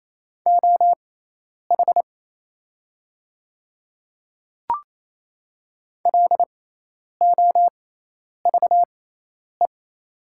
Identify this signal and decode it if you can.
Morse code, heard as a keyed tone, about 28 wpm, spaced as for 10 wpm, 700 Hz. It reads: OH LOVE
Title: 6x Character Spacing - Top 500 Words - 28wpm